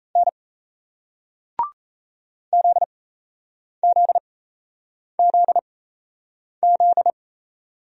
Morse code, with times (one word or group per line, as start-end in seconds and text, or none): 0.2-0.3 N
2.5-7.1 ZZZZ